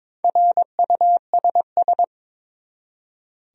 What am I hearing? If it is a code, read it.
Morse code, heard as a keyed tone, about 22 wpm, 700 Hz. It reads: RUSS
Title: USA Common Names - Max 4 Letters - 22wpm